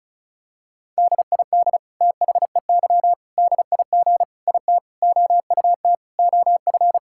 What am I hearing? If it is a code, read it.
Morse code, heard as a keyed tone, about 34 wpm, 700 Hz. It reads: DID THEY DIG IT OUT OF
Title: Sentences from Top 1600 Words 35wpm